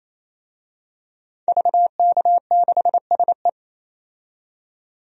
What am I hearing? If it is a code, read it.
Morse code, heard as a keyed tone, about 28 wpm, 700 Hz. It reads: VK6SE
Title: DXCC-100 Call Signs - 5 Characters - 28wpm